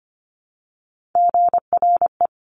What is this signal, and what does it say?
Morse code, heard as a keyed tone, about 25 wpm, 700 Hz. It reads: GRE
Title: Top Three Letter Combinations in Words 25wpm